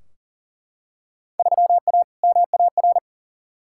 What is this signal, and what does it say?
Morse code, heard as a keyed tone, about 40 wpm, 700 Hz. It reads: 3A MAR
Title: Field Day Exchanges Encore - 40wpm